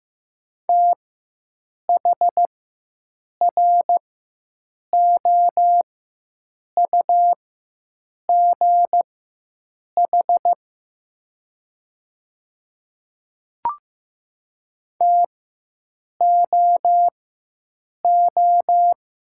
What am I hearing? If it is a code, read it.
Morse code, heard as a keyed tone, about 15 wpm, spaced as for 7 wpm, 700 Hz. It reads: THROUGH TOO
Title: 4x Character Spacing - Top 300 Words - 15wpm